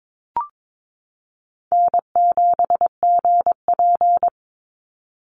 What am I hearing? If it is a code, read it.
Morse code, heard as a keyed tone, about 22 wpm, 700 Hz. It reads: N7GP
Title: USA Call Signs 22wpm